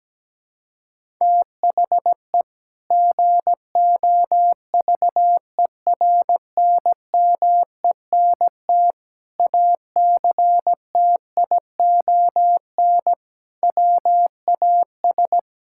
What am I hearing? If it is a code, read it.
Morse code, heard as a keyed tone, about 17 wpm, 700 Hz. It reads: THE GOVERNMENT ACTION WAS
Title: Sentences from Top 1000 Words - Review Part 5 - 17wpm